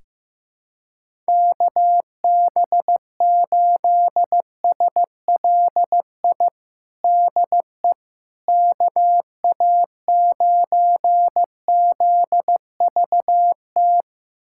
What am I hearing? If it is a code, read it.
Morse code, heard as a keyed tone, about 15 wpm, 700 Hz. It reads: KB8SLI DE KA9ZVT
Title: USA Callsign DE Callsign - 6 Characters - 15wpm